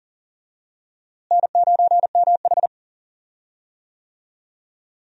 Morse code, heard as a keyed tone, about 40 wpm, 700 Hz. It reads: N9MH
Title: USA Call Signs - 4 Characters - Repeated 3x - 40wpm